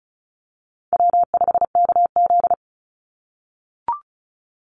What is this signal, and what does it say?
Morse code, heard as a keyed tone, about 34 wpm, 700 Hz. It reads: W5KZ
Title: USA Call Signs 35wpm